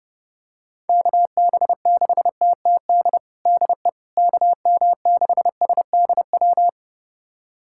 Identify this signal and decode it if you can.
Morse code, heard as a keyed tone, about 30 wpm, 700 Hz. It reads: KB6TTD DE KM6SDW
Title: USA Callsign DE Callsign - 6 Characters - 30wpm